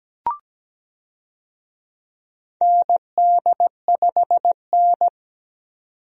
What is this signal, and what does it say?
Morse code, heard as a keyed tone, about 17 wpm, 700 Hz. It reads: ND5N
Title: USA Call Signs - 4 Characters - 17wpm